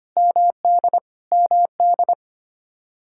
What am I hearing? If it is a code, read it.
Morse code, heard as a keyed tone, about 25 wpm, 700 Hz. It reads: MD MD
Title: Practice QSO - Exchange 1 - Beginning - 25wpm